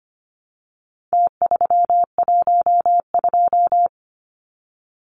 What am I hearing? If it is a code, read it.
Morse code, heard as a keyed tone, about 25 wpm, 700 Hz. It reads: T312
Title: Numbers - Contest Style - 4-Digit 25wpm